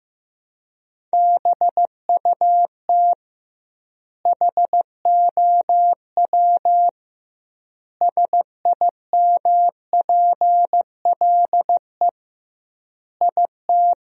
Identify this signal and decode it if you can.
Morse code, heard as a keyed tone, about 15 wpm, 700 Hz. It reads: BUT HOW SIMPLE IT
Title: 2x Word-Spacing - Sentences from Top 400 Words - 15wpm